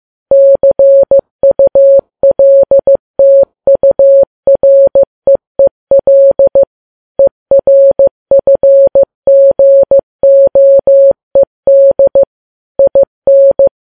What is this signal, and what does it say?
Morse code, heard as a keyed tone, about 15 wpm, 550 Hz. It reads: CULTUREEL ERFGOED IN
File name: Files Web tls / Morse on9fv / morse15kps